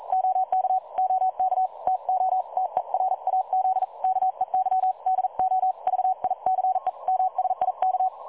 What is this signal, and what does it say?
Morse code, heard as a keyed tone, about 42 wpm, 750 Hz. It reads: OK OK TO TERAZ KEY DO VICE MSIK
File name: dlazaawansowanych / nagraniazpasma / websdr recording 3565.0kHz